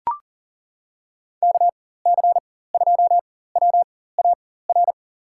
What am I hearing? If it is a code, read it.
Morse code, heard as a keyed tone, about 39 wpm, spaced as for 19 wpm, 700 Hz. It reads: KC2WAR
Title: USA Call Signs - 6 Characters - 4x Character Spacing - 40wpm